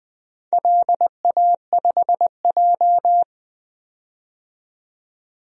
Morse code, heard as a keyed tone, about 20 wpm, 700 Hz. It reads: LA5J
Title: DXCC-100 Call Signs - 4 Characters - 20wpm